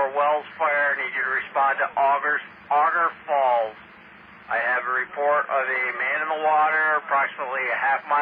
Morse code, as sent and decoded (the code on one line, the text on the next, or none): none